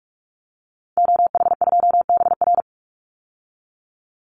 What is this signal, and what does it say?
Morse code, heard as a keyed tone, about 44 wpm, 700 Hz. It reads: OH2BR
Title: Call Signs - 45wpm